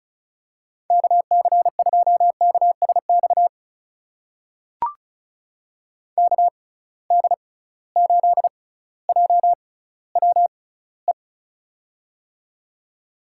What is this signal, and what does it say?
Morse code, heard as a keyed tone, about 36 wpm, spaced as for 18 wpm, 700 Hz. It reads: KC2KSX KD8JWE
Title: USA Call Signs - 6 Characters - 6x Character Spacing - 35wpm